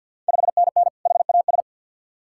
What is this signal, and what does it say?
Morse code, heard as a keyed tone, about 51 wpm, 690 Hz. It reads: 5NN HAS
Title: Ontario QSO Party Exchange 50wpm